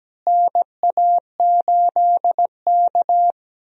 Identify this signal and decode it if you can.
Morse code, heard as a keyed tone, about 17 wpm, 700 Hz. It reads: NA8K
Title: USA Call Signs - 4 Characters - Repeated 3x - 17wpm